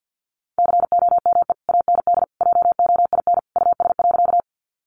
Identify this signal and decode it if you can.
Morse code, heard as a keyed tone, about 50 wpm, 700 Hz. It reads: COME AND JOIN US.
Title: Sentences from Top 1400 Words 50wpm